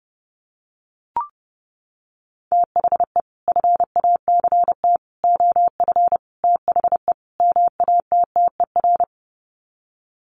Morse code, heard as a keyed tone, about 30 wpm, 700 Hz. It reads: THE FACT OF THE MATTER
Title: Sets of 5 Words 30wpm